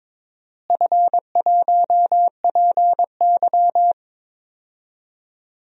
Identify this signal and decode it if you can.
Morse code, heard as a keyed tone, about 22 wpm, 700 Hz. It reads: F1PY
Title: DXCC-100 Call Signs - 4 Characters - Repeated 3x - 22wpm